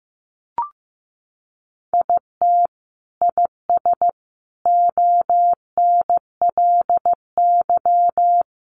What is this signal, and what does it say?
Morse code, heard as a keyed tone, about 15 wpm, 700 Hz. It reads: IT IS ONLY
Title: Sentences from Top 700 Words 15wpm